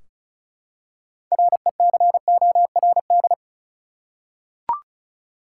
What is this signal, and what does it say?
Morse code, heard as a keyed tone, about 34 wpm, 700 Hz. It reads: RECORD
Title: Six Letter Words - 35wpm